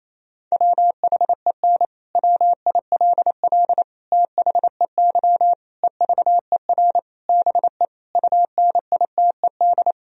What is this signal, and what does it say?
Morse code, heard as a keyed tone, about 28 wpm, 700 Hz. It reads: WHEN WILL THEY EVER BE UNITED
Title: Sentences from Top 1000 Words - Review Part 4 - 28wpm